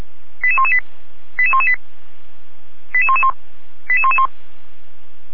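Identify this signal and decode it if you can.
Five-tone selective call (ZVEI), heard as quick runs of five tones, tones of about 70 ms. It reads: 88118 88118 88111 88111
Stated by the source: ZVEI five-tone selective call received on SDR, address 88497, from feuerwehr-orlamuende fire service.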